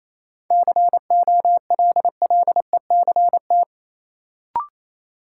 Morse code, heard as a keyed tone, about 28 wpm, 700 Hz. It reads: COLLECT
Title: Sentences from Top 1600 Words - New Word Drill Alt - 28wpm